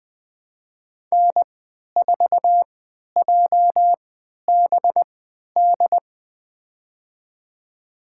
Morse code, heard as a keyed tone, about 20 wpm, spaced as for 11 wpm, 700 Hz. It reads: N4JBD
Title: USA Call Signs - 3x Character Spacing - 20wpm